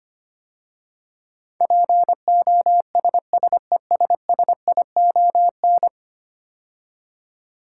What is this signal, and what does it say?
Morse code, heard as a keyed tone, about 25 wpm, 700 Hz. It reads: POSSESSION
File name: 10 Letter Words - First 3 Letters Given - 25wpm